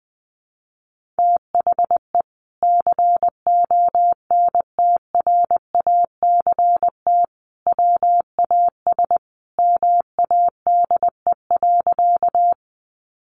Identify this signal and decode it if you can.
Morse code, heard as a keyed tone, about 20 wpm, 700 Hz. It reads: THE CONTRACT WAS MADE.
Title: Sentences from Top 1400 Words 20wpm